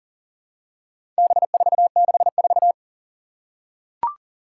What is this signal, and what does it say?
Morse code, heard as a keyed tone, about 40 wpm, 700 Hz. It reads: B464
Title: Four Challenging Letter-Number Combinations 40wpm